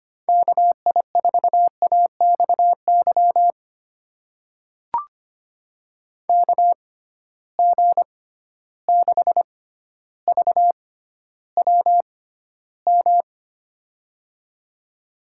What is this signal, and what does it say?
Morse code, heard as a keyed tone, about 25 wpm, spaced as for 13 wpm, 700 Hz. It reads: KI4AXY KG6VWM
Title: USA Call Signs - 6 Characters - 6x Character Spacing - 25wpm